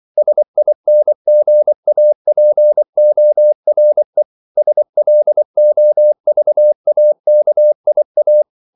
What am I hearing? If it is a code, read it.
Morse code, heard as a keyed tone, about 24 wpm, 600 Hz. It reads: SINGAPORE SLOVAKIA
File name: CW / Practice / NationalCapitals24